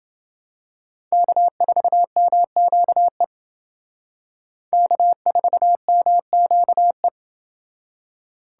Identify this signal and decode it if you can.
Morse code, heard as a keyed tone, about 28 wpm, 700 Hz. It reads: K4MQE K4MQE